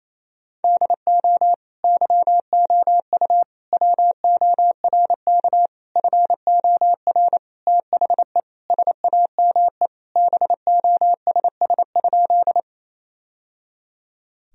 Morse code, heard as a keyed tone, about 28 wpm, 700 Hz. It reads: DO YOU WORK FOR THE SAME BOSS?
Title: Sentences from Top 1000 Words - Review Part 4 - 28wpm